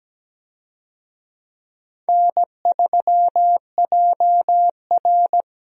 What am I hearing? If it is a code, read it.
Morse code, heard as a keyed tone, about 17 wpm, 700 Hz. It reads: N3JR
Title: USA Call Signs - 4 Characters - Repeated 3x - 17wpm